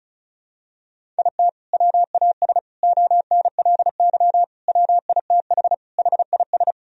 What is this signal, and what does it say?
Morse code, heard as a keyed tone, about 34 wpm, 700 Hz. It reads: IT WAS ONLY WITH HIS
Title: Sentences from Top 1000 Words - Review Part 1 - 35wpm